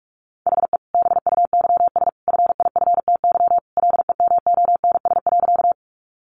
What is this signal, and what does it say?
Morse code, heard as a keyed tone, about 44 wpm, 700 Hz. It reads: HE BUYS FIFTY LEMONS.
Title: Additional Easy Phrases 45wpm